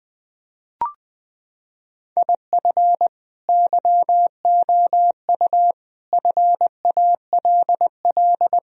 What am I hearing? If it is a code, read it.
Morse code, heard as a keyed tone, about 21 wpm, 700 Hz. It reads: IF YOU FALL
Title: Sentences from Top 1000 Words - Review Part 2 - 20wpm